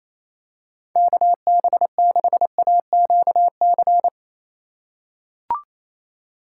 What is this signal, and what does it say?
Morse code, heard as a keyed tone, about 28 wpm, 700 Hz. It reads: KB6AQC
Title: USA Call Signs - 6 Characters - 3x Character Spacing - 28wpm